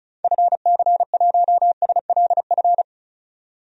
Morse code, heard as a keyed tone, about 34 wpm, 700 Hz. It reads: FC1SLF
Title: DXCC-100 Call Signs - 6 Characters - 35wpm